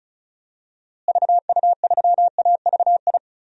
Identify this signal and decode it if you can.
Morse code, heard as a keyed tone, about 34 wpm, 700 Hz. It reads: VU3AVI
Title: DXCC-100 Call Signs - 6 Characters - Repeated 2x - 35wpm